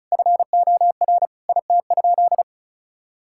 Morse code, heard as a keyed tone, about 34 wpm, 700 Hz. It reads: FOR IT?
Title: Sentences from Top 1000 Words - Review Part 6 - 35wpm